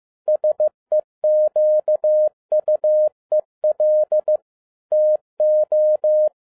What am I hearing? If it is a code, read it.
Morse code, heard as a keyed tone, about 15 wpm, 610 Hz. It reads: SEQUEL TO
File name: 2012-01-mp3 / n13-A NEWLY DESIGNED QSL CARD-15wpm